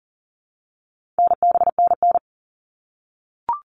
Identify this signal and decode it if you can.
Morse code, heard as a keyed tone, about 37 wpm, 700 Hz. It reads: NBNN
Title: All Cut Numbers - 4-Digit 40wpm